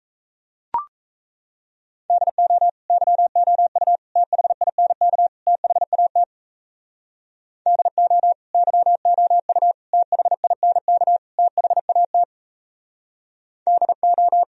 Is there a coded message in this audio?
Yes: Morse code from 2.1 to 14.4 s, heard as a keyed tone, about 35 wpm, 700 Hz. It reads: DO YOU THINK THAT DO YOU THINK THAT DO